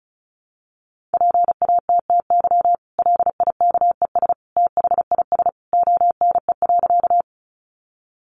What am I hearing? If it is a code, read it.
Morse code, heard as a keyed tone, about 34 wpm, 700 Hz. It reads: PATTY LIKES THIS ONE.
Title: Additional Easy Phrases 35wpm